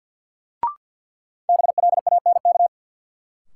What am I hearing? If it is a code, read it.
Morse code, heard as a keyed tone, about 49 wpm, 700 Hz. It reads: BLANK
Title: Sentences from Top 1500 Words - New Word Drill - 50wpm